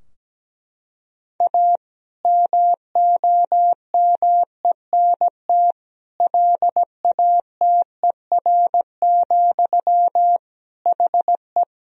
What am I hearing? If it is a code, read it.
Morse code, heard as a keyed tone, about 17 wpm, 700 Hz. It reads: A MOMENT LATER, HE
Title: Sentences from Top 1000 Words - Review Part 4 - 17wpm